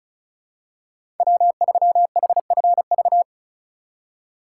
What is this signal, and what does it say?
Morse code, heard as a keyed tone, about 36 wpm, 700 Hz. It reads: W3HFV
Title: USA Call Signs - 6x Character Spacing - 35wpm